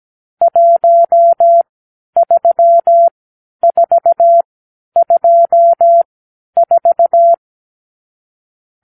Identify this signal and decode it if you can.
Morse code, heard as a keyed tone, about 17 wpm, spaced as for 10 wpm, 680 Hz. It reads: 13424
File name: k7qo / Lesson 50 (1-6)